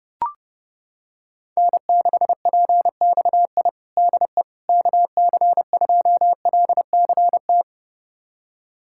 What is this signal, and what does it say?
Morse code, heard as a keyed tone, about 30 wpm, 700 Hz. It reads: N6PXI DE KC2LCT
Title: USA Callsign DE Callsign 30wpm